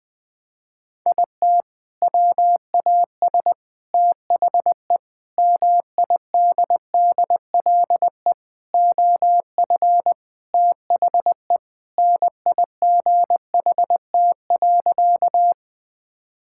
Morse code, heard as a keyed tone, about 20 wpm, 700 Hz. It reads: IT WAS THE MIDDLE OF THE NIGHT.
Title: Sentences from Top 1000 Words - Review Part 6 - 20wpm